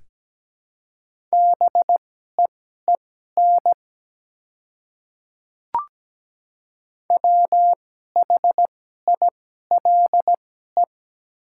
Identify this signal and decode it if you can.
Morse code, heard as a keyed tone, about 17 wpm, spaced as for 12 wpm, 700 Hz. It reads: BEEN WHILE
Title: 2x Character Spacing - Top 300 Words - 17wpm